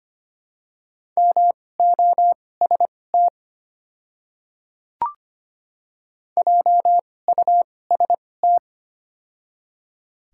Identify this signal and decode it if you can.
Morse code, heard as a keyed tone, about 25 wpm, spaced as for 18 wpm, 700 Hz. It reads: MOST JUST
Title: 2x Character Spacing - Top 200 Words - 25wpm